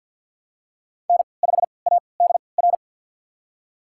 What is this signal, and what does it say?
Morse code, heard as a keyed tone, about 52 wpm, spaced as for 29 wpm, 700 Hz. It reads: N5ADR
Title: USA Call Signs - 5 Characters - 3x Character Spacing - 50wpm